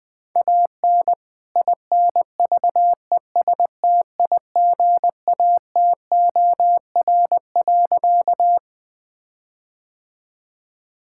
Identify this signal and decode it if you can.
Morse code, heard as a keyed tone, about 20 wpm, 700 Hz. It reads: AN INVESTIGATOR.
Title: All-Age Appropriate Jokes - 20wpm